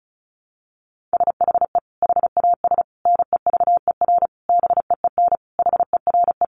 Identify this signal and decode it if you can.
Morse code, heard as a keyed tone, about 34 wpm, 700 Hz. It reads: SHE HAS NEVER BEEN HERE
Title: Additional Easy Phrases 35wpm